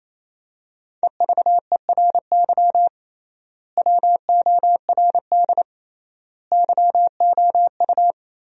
Morse code, heard as a keyed tone, about 28 wpm, 700 Hz. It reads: EVERY WORD YOU
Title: 3x Word-Spacing - Sentences from Top 400 Words - 28wpm